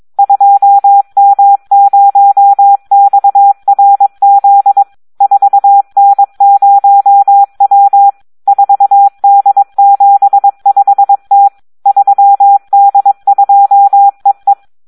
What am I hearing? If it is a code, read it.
Morse code, heard as a keyed tone, about 22 wpm, 800 Hz. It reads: 2M0XRZ 4N0W 4D75T 3D2EE
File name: Audio practica CW / SD 1 22